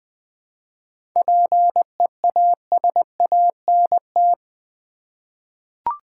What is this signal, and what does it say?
Morse code, heard as a keyed tone, about 20 wpm, 700 Hz. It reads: PEASANT
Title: 7 Letter Words - First 3 Letters Given - 20wpm